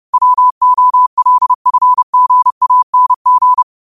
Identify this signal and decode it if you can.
Morse code, heard as a keyed tone, about 30 wpm, 1000 Hz. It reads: WOLFGANG
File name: qotdfiles / 30wpm